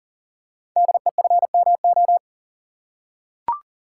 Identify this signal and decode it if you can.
Morse code, heard as a keyed tone, about 39 wpm, 700 Hz. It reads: DEFMO
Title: Code Groups - 5 Letters - 40wpm